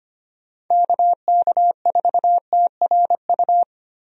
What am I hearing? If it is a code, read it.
Morse code, heard as a keyed tone, about 25 wpm, 700 Hz. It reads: KK4TRU
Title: USA Call Signs - 6 Characters - Repeated 3x - 25wpm